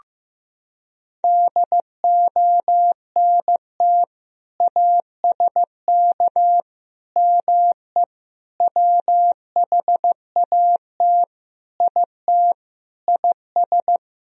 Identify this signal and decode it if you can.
Morse code, heard as a keyed tone, about 15 wpm, 700 Hz. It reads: DONT ASK ME WHAT IT IS
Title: Sentences from Top 1000 Words - Review Part 3 - 15wpm